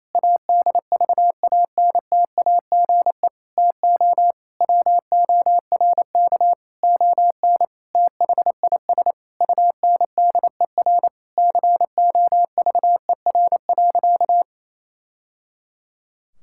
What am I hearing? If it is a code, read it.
Morse code, heard as a keyed tone, about 28 wpm, 700 Hz. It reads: ADVANTAGE TO WORK ON THIS UNDER COVER.
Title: Sentences from Top 1000 Words - Review Part 6 - 28wpm